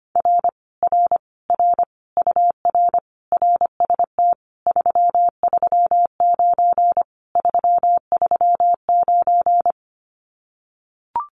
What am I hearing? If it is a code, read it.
Morse code, heard as a keyed tone, about 25 wpm, 700 Hz. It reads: R R R UR RST 339 339
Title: Common QSO Phrases Encore 25wpm